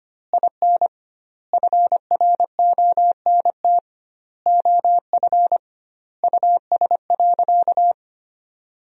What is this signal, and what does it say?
Morse code, heard as a keyed tone, about 25 wpm, 700 Hz. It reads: IN FRONT OF US.
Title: 2x Word-Spacing - Sentences from Top 500 Words - 25wpm